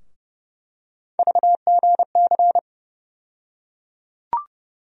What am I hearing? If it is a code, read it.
Morse code, heard as a keyed tone, about 29 wpm, 700 Hz. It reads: VGC
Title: USA Call Sign Postfixes - 3 Letters 30wpm